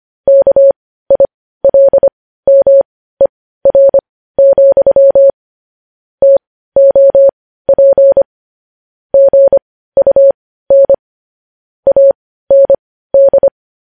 Morse code, heard as a keyed tone, about 25 wpm, spaced as for 15 wpm, 550 Hz. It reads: KILMER, TOP GUN AND
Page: news headlines in morse code at 15 wpm